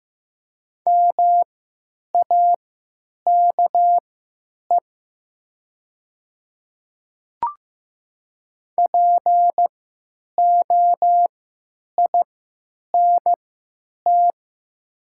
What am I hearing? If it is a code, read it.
Morse code, heard as a keyed tone, about 15 wpm, spaced as for 9 wpm, 700 Hz. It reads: MAKE POINT
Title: 3x Character Spacing - Top 200 Words - 15wpm